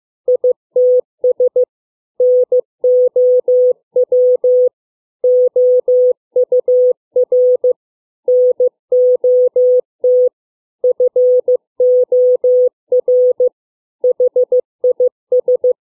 Morse code, heard as a keyed tone, about 15 wpm, 500 Hz. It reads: ITS NOW OUR NOT FOR HIS
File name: CW / fixed length words 4